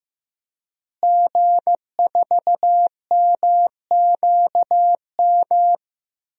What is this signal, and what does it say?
Morse code, heard as a keyed tone, about 16 wpm, 700 Hz. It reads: G4MQM
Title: DXCC-100 Call Signs - 5 Characters - Repeated 3x - 15wpm